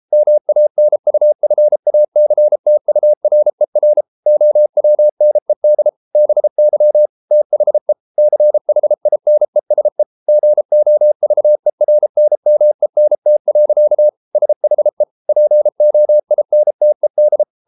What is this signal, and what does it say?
Morse code, heard as a keyed tone, about 33 wpm, 600 Hz. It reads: MANUFACTURER OWNED BY THE CHINESE GOVERNMENT. SHE POINTED